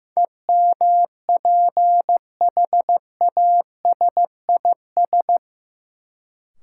Morse code, heard as a keyed tone, about 15 wpm, 700 Hz. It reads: EMPHASIS
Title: Sentences from Top 1400 Words - New Word Drill - 15wpm